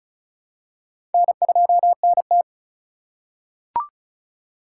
Morse code, heard as a keyed tone, about 36 wpm, 700 Hz. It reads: N2NT
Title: USA Call Signs - 4 Characters - 3x Character Spacing - 35wpm